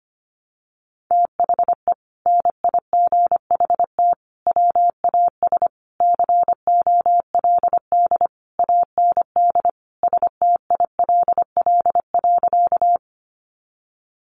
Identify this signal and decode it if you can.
Morse code, heard as a keyed tone, about 25 wpm, 700 Hz. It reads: THE NIGHT WAS COLD AND STILL.
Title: Sentences from Top 400 Words 25wpm